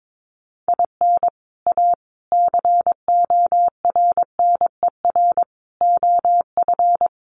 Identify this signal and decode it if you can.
Morse code, heard as a keyed tone, about 22 wpm, 700 Hz. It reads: IN A CORNER OF
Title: Sets of 5 Words Encore2 22wpm